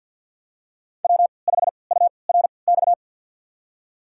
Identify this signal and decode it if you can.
Morse code, heard as a keyed tone, about 51 wpm, spaced as for 28 wpm, 700 Hz. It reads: W5URX
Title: USA Call Signs - 5 Characters - 3x Character Spacing - 50wpm